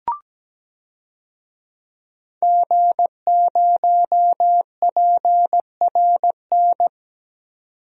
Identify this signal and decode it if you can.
Morse code, heard as a keyed tone, about 17 wpm, 700 Hz. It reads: G0PRN
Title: DXCC-100 Call Signs - 5 Characters - 17wpm